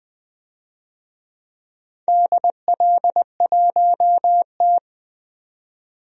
Morse code, heard as a keyed tone, about 20 wpm, 700 Hz. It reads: DL1T
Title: DXCC-100 Call Signs - 4 Characters - 20wpm